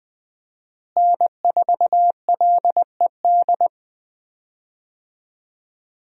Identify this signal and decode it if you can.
Morse code, heard as a keyed tone, about 20 wpm, 700 Hz. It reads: N4LED
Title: USA Call Signs - 5 Characters - Repeated 2x - 20wpm